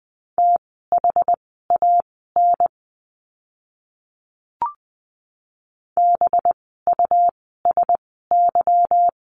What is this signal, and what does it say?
Morse code, heard as a keyed tone, about 20 wpm, spaced as for 15 wpm, 700 Hz. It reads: THAN BUSY
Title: 185 Lesson 30f - Letter b - Words 20wpm